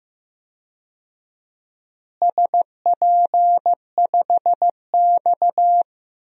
Morse code, heard as a keyed tone, about 15 wpm, 700 Hz. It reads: SP5X
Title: DXCC-100 Call Signs - 4 Characters - Repeated 2x - 15wpm